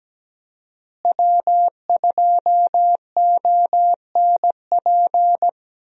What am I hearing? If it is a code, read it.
Morse code, heard as a keyed tone, about 17 wpm, 700 Hz. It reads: W2ONP
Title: USA Call Signs - 5 Characters - Repeated 3x - 17wpm